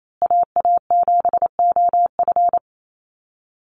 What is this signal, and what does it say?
Morse code, heard as a keyed tone, about 28 wpm, 700 Hz. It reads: AA7OF
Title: USA Call Signs Repeated 3x 28wpm